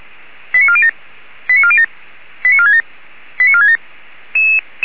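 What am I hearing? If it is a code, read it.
Five-tone selective call (ZVEI), heard as quick runs of five tones, tones of about 70 ms. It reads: 89407 89407 89467 89467
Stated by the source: SDR receiver recording of ZVEI five-tone selective call 89406.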